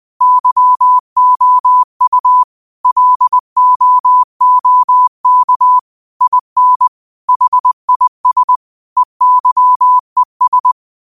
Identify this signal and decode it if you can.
Morse code, heard as a keyed tone, about 20 wpm, 1000 Hz. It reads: YOU LOOK IN HIS EYES